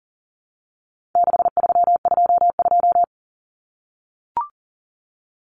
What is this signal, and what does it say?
Morse code, heard as a keyed tone, about 40 wpm, 700 Hz. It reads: 6322
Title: Numbers - 4-Digit 40wpm